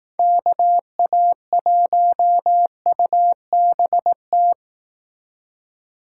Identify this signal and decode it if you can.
Morse code, heard as a keyed tone, about 18 wpm, 700 Hz. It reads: KA1UBT
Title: Speed Racing - USA 2x3 Call Signs 15wpm